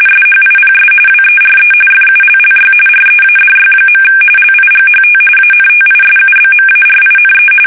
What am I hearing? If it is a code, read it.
RTTY, about 45.45 baud. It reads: SHIPS WERE FULLY READY. LIGHT ANTISUBMARINE
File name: fsk850